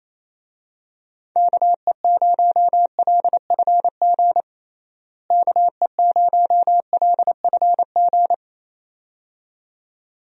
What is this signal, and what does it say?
Morse code, heard as a keyed tone, about 28 wpm, 700 Hz. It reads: KE0LFG KE0LFG
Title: USA Call Signs - 6 Characters - Repeated 2x - 28wpm